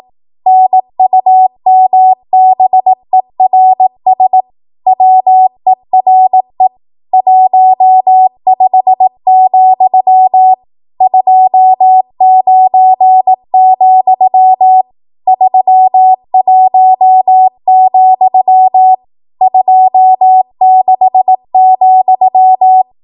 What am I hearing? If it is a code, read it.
Morse code, heard as a keyed tone, about 18 wpm, 750 Hz. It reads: NUMBERS WERE 15, 29, 31, 26,
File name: Archive / 18 WPM / 18WPM